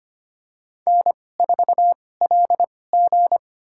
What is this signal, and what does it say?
Morse code, heard as a keyed tone, about 25 wpm, spaced as for 18 wpm, 700 Hz. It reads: N4LG
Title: USA Call Signs - 2x Character Spacing - 25wpm